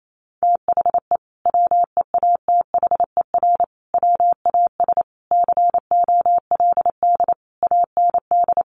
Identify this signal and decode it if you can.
Morse code, heard as a keyed tone, about 28 wpm, 700 Hz. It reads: THE WEATHER WAS COLD AND
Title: Sentences from Top 1500 Words 28wpm